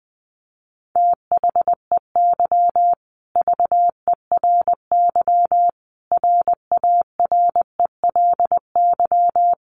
Morse code, heard as a keyed tone, about 20 wpm, 700 Hz. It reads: THEY VERY RARELY
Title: Sentences from Top 1400 Words 20wpm